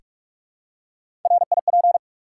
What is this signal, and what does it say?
Morse code, heard as a keyed tone, about 44 wpm, 700 Hz. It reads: RIP
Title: Three Letter Words - 45wpm